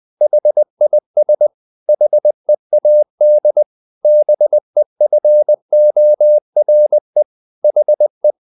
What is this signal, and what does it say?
Morse code, heard as a keyed tone, about 20 wpm, 600 Hz. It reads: HIS HEAD BEFORE HE
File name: the old man and the sea / the old man and the sea part 1 20wpm